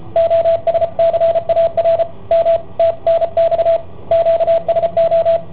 Morse code, heard as a keyed tone, about 33 wpm, 670 Hz, with radio noise. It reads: OSCAR M TNX QSO
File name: i7ohp / cassetta i7ohp B